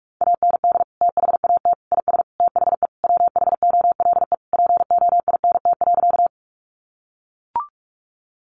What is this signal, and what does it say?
Morse code, heard as a keyed tone, about 44 wpm, 700 Hz. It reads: AND THAT IS THE WHOLE POINT.
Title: Sentences from Top 400 Words 45wpm